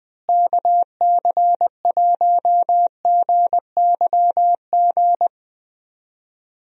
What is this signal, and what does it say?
Morse code, heard as a keyed tone, about 20 wpm, 700 Hz. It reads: KC1GYG
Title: USA Call Signs - 6 Characters - Repeated 2x - 20wpm